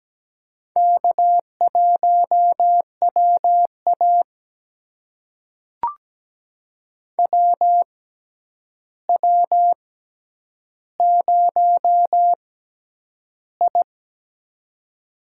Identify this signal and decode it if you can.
Morse code, heard as a keyed tone, about 17 wpm, spaced as for 9 wpm, 700 Hz. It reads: K1WA WW0I
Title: USA Call Signs - 4 Characters - 6x Character Spacing - 17wpm